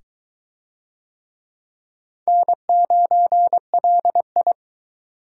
Morse code, heard as a keyed tone, about 21 wpm, 700 Hz. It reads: N9LI